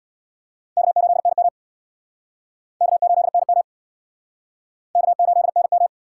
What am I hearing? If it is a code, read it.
Morse code, heard as a keyed tone, about 68 wpm, 700 Hz. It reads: K8NM K8NM K8NM